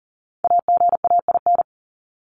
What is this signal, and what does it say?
Morse code, heard as a keyed tone, about 40 wpm, 700 Hz. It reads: AGAIN
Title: Top 200 Words 40wpm